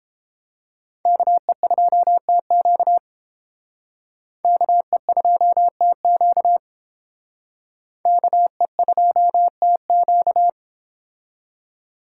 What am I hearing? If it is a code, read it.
Morse code, heard as a keyed tone, about 28 wpm, 700 Hz. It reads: KE2TQ KE2TQ KE2TQ